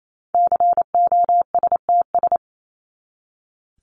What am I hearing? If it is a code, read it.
Morse code, heard as a keyed tone, about 28 wpm, 700 Hz. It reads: COSTS
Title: ES:S Suffix - Top 100 Words with Base Words 28wpm